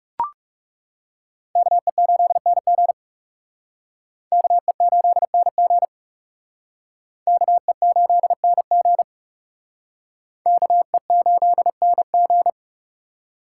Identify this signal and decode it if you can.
Morse code, heard as a keyed tone, about 40 wpm, 700 Hz. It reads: KE8NG KE8NG KE8NG KE8NG